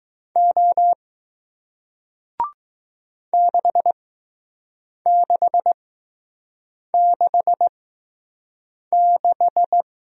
Morse code, heard as a keyed tone, about 20 wpm, spaced as for 6 wpm, 700 Hz. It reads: O 6666